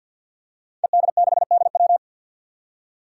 Morse code, heard as a keyed tone, about 51 wpm, 700 Hz. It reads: ED6DW
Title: Five Challenging Letter-Number Combinations 50wpm